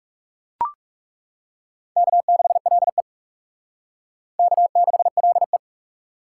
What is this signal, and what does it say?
Morse code, heard as a keyed tone, about 40 wpm, 700 Hz. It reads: K6LE K6LE